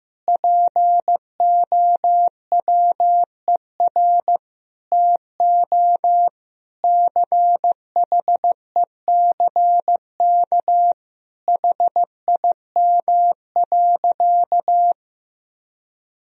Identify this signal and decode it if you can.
Morse code, heard as a keyed tone, about 15 wpm, 700 Hz. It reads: POWER TO CHECK HIM.
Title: Sentences from Top 1000 Words - Review Part 3 - 15wpm